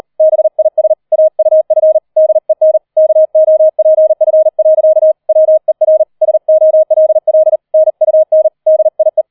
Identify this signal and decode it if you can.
Morse code, heard as a keyed tone, about 38 wpm, 630 Hz. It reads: BIS AUF DEN KOPF. WER SOLL NUN DIE
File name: 004 Vierter Streich 40wpm